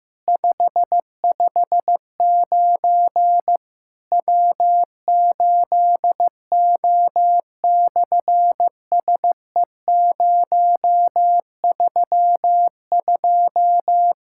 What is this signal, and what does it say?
Morse code, heard as a keyed tone, about 15 wpm, 700 Hz. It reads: 559 W8O/SE032
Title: SOTA QSO Practice 15wpm